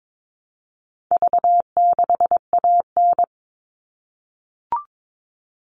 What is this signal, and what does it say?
Morse code, heard as a keyed tone, about 22 wpm, 700 Hz. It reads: V6AN
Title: All Cut Numbers - 4-Digit 22wpm